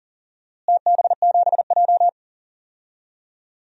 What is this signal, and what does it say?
Morse code, heard as a keyed tone, about 40 wpm, 700 Hz. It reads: TB7J
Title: DXCC-100 Call Signs - 4 Characters - Repeated 3x - 40wpm